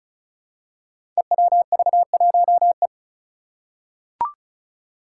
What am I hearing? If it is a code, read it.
Morse code, heard as a keyed tone, about 36 wpm, 700 Hz. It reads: EWV1E
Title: Five Challenging Letter-Number Combinations 35wpm